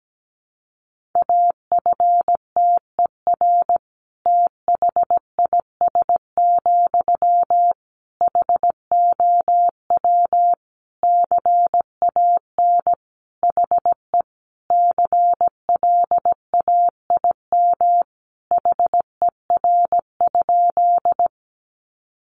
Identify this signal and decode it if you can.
Morse code, heard as a keyed tone, about 17 wpm, 700 Hz. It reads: AFTER THIS, HOW CAN HE CLAIM HER?
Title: Sentences from Top 1500 Words 17wpm